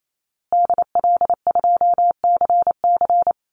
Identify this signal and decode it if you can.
Morse code, heard as a keyed tone, about 28 wpm, 700 Hz. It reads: DL2CC
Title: Call Signs Repeated 2x - 28wpm